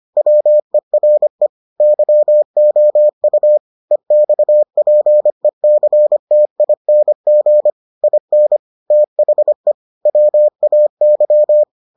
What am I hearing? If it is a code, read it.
Morse code, heard as a keyed tone, about 25 wpm, 600 Hz. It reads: WERE YOU EXPECTING IN THE WAY